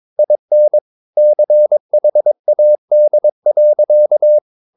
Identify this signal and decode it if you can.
Morse code, heard as a keyed tone, about 22 wpm, 600 Hz. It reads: IN CHAD.
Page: Romanen Never af Ken Follet på engelsk.